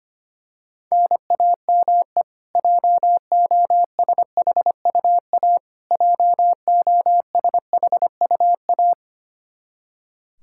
Morse code, heard as a keyed tone, about 25 wpm, 700 Hz. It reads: NAME JOSHUA JOSHUA
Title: Practice QSO - Exchange 1 - Beginning - 25wpm